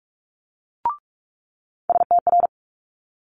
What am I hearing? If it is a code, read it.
Morse code, heard as a keyed tone, about 46 wpm, 700 Hz. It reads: STR
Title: Multi-letter Phonemes Quick Fire 45wpm